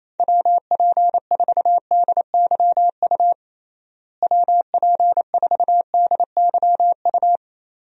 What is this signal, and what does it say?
Morse code, heard as a keyed tone, about 28 wpm, 700 Hz. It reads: WP4DYU WP4DYU
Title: USA Call Signs - 6 Characters - Repeated 2x - 28wpm